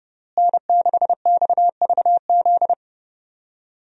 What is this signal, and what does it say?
Morse code, heard as a keyed tone, about 30 wpm, 700 Hz. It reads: N6XVZ
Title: USA Call Signs - 5 Characters - Repeated 3x - 30wpm